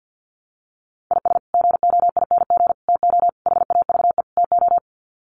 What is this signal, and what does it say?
Morse code, heard as a keyed tone, about 50 wpm, 700 Hz. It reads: IS GOING TO HAVE TO
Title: Sets of 5 Words 50wpm